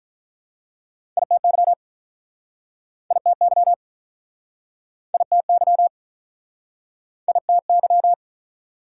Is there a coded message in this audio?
Yes: Morse code from 1.2 to 8.1 s, heard as a keyed tone, about 50 wpm, 700 Hz. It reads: ITY ITY ITY ITY